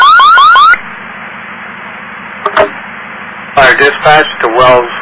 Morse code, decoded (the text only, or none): none